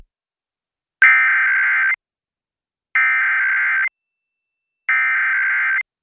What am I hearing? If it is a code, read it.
Emergency Alert System (SAME) tones (header bursts), heard as harsh buzzing bursts. It is heard from 1.0 to 5.8 s.